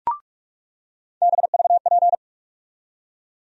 Morse code, heard as a keyed tone, about 46 wpm, 700 Hz. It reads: BVP
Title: USA Call Sign Postfixes - 3 Letters 45wpm